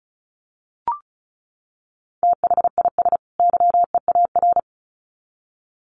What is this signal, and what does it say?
Morse code, heard as a keyed tone, about 34 wpm, 700 Hz. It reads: THIS YEAR
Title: Sets of 2 Words 35wpm